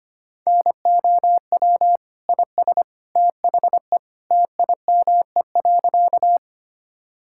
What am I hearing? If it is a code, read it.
Morse code, heard as a keyed tone, about 25 wpm, 700 Hz. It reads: NOW IS THE TIME.
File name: practice / 25wpm / Sentences with 2-Word Phrases - 25wpm